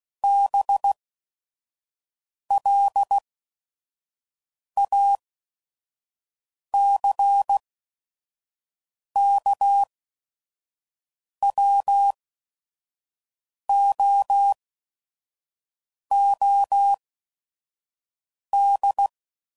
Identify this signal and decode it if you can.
Morse code, heard as a keyed tone, about 16 wpm, spaced as for 5 wpm, 800 Hz. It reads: BLACKWOOD
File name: CW Code Groups @5WPM / fromSM's / Cities